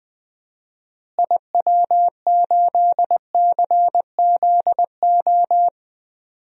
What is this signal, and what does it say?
Morse code, heard as a keyed tone, about 21 wpm, 700 Hz. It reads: IW8CZO
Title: DXCC-100 Call Signs - 6 Characters - Repeated 2x - 20wpm